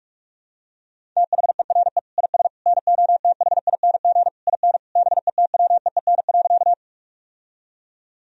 Morse code, heard as a keyed tone, about 41 wpm, 700 Hz. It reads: THERE IS NOTHING IN BETWEEN.
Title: Sentences from Top 1000 Words - Review Part 1 - 45wpm